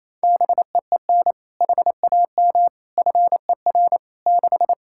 Morse code, heard as a keyed tone, about 28 wpm, 700 Hz. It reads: BEEN HAM FER 6
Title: Practice QSO - Exchange 3 - More Info - 28wpm